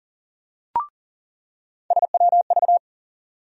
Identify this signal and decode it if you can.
Morse code, heard as a keyed tone, about 40 wpm, 700 Hz. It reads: SWV
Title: USA Call Sign Postfixes - 3 Letters 40wpm